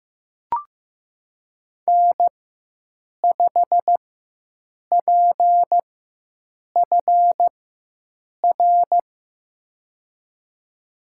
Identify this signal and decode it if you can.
Morse code, heard as a keyed tone, about 15 wpm, spaced as for 7 wpm, 700 Hz. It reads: N5PFR
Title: USA Call Signs - 5 Characters - 4x Character Spacing - 15wpm